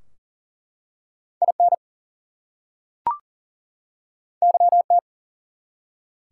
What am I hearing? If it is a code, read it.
Morse code, heard as a keyed tone, about 40 wpm, 700 Hz. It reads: IN YT
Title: US States and Canadian Provinces 40wpm